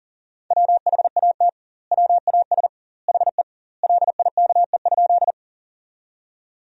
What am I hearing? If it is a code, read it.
Morse code, heard as a keyed tone, about 40 wpm, 700 Hz. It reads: WHAT WAS HE LIKE?
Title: 2x Word-Spacing - Sentences from Top 100 Words - 40wpm